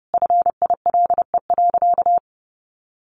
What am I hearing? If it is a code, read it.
Morse code, heard as a keyed tone, about 30 wpm, 700 Hz. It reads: FILE.
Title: Sentences from Top 1400 Words 30wpm